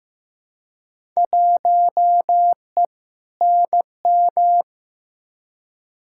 Morse code, heard as a keyed tone, about 15 wpm, 700 Hz. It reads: 1E NM
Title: Field Day Exchanges - 15wpm